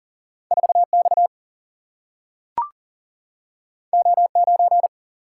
Morse code, heard as a keyed tone, about 39 wpm, 700 Hz. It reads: 4 X O 9
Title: Single-Letter Single-Number 40wpm